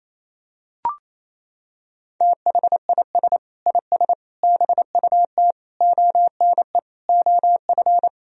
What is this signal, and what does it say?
Morse code, heard as a keyed tone, about 27 wpm, 700 Hz. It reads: THIS IS BUT ONE OF
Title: Sentences from Top 1000 Words - Review Part 2 - 28wpm